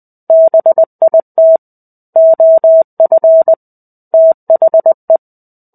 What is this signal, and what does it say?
Morse code, heard as a keyed tone, about 20 wpm, 650 Hz. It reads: BIT OF THE
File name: 05-Chapter-05 20 WPM